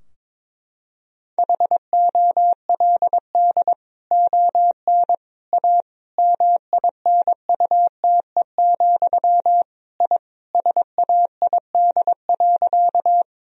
Morse code, heard as a keyed tone, about 22 wpm, 700 Hz. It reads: HOLD ON A MINUTE, I SAID.
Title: Sentences from Top 1000 Words - Review Part 2 - 22wpm